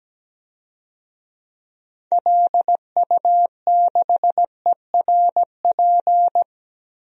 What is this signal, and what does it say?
Morse code, heard as a keyed tone, about 17 wpm, 700 Hz. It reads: LU6ERP
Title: DXCC-100 Call Signs - 6 Characters - Repeated 2x - 17wpm